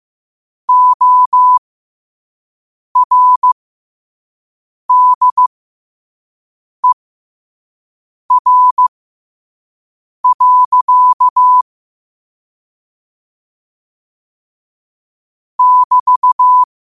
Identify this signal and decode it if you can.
Morse code, heard as a keyed tone, about 15 wpm, spaced as for 5 wpm, 1000 Hz. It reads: ORDER. <BT>
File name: qotdfiles / 5wpm